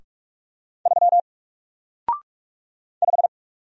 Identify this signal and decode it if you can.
Morse code, heard as a keyed tone, about 47 wpm, 700 Hz. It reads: 3 5
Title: Single Character (Letter:Number:Punctuation:Prosign) Alt 45wpm